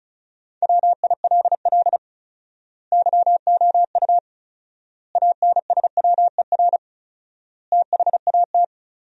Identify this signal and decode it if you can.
Morse code, heard as a keyed tone, about 36 wpm, 700 Hz. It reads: WILL YOU ANSWER THAT
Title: 4x Word-Spacing - Sentences from Top 400 Words - 35wpm